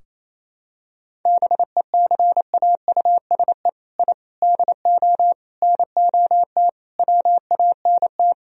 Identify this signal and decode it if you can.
Morse code, heard as a keyed tone, about 28 wpm, 700 Hz. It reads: BECAUSE I DO NOT WANT
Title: Sentences from Top 1000 Words - Review Part 3 - 28wpm